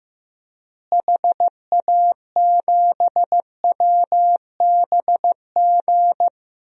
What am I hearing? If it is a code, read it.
Morse code, heard as a keyed tone, about 15 wpm, 700 Hz. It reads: HA7WBG
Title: DXCC-100 Call Signs - 6 Characters - Repeated 2x - 15wpm